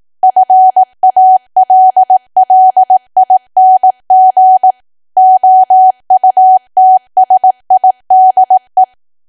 Morse code, heard as a keyed tone, about 18 wpm, 750 Hz. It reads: FALLING OUTSIDE
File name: cwa / W1AW-Level2 / 18WPM